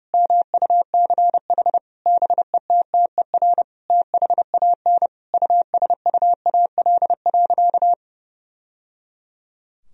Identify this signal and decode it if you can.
Morse code, heard as a keyed tone, about 32 wpm, 700 Hz. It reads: MUCH BETTER THAN USUAL.
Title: Sentences from Top 1000 Words - Review Part 2 - 30wpm